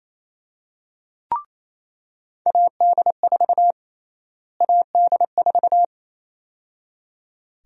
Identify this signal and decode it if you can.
Morse code, heard as a keyed tone, about 28 wpm, 700 Hz. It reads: AD4 AD4
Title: USA Call Sign Prefixes - Repeated 2x - 28wpm